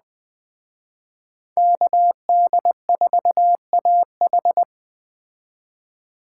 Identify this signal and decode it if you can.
Morse code, heard as a keyed tone, about 20 wpm, 700 Hz. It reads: KD4AH
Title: Speed Racing - USA 2x2 Call Signs 17wpm